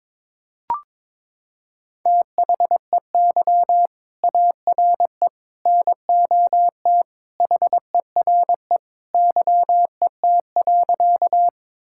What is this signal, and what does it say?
Morse code, heard as a keyed tone, about 21 wpm, 700 Hz. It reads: THEY ARE NOT HERE YET.
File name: practice / 22wpm / Sentences with 2-Word Phrases - 22wpm